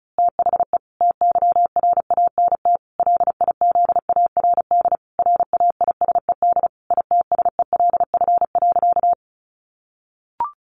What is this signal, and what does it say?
Morse code, heard as a keyed tone, about 34 wpm, 700 Hz. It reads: THE TYRANT LIZARD RAISED ITSELF.
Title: A Sound Of Thunder 35wpm